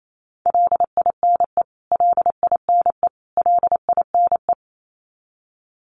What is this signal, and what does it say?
Morse code, heard as a keyed tone, about 28 wpm, 700 Hz. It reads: LINE LINE LINE
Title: Top 500 Words Repeated 28wpm